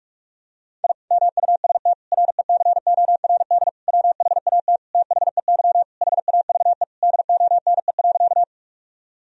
Morse code, heard as a keyed tone, about 46 wpm, 700 Hz. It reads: I MUST RECORD WHAT THEY HAVE DONE.
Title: Sentences from Top 1000 Words - Review Part 3 - 45wpm